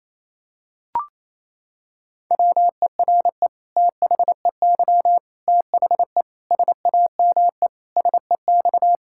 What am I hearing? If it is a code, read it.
Morse code, heard as a keyed tone, about 27 wpm, 700 Hz. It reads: WERE THEY THE SAME SEX
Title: Sentences from Top 1600 Words 28wpm